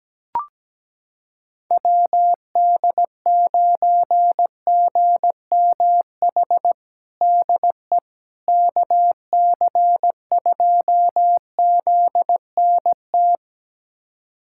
Morse code, heard as a keyed tone, about 17 wpm, 700 Hz. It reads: WD9GMH DE KC2ZNT
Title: USA Callsign DE Callsign 17wpm